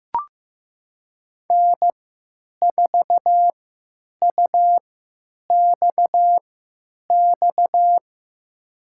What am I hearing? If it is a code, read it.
Morse code, heard as a keyed tone, about 15 wpm, spaced as for 9 wpm, 700 Hz. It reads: N4UXX
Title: USA Call Signs - 5 Characters - 3x Character Spacing - 15wpm